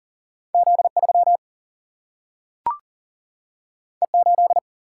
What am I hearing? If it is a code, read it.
Morse code, heard as a keyed tone, about 40 wpm, 700 Hz. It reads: Z3 E8
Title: Single-Letter Single-Number 40wpm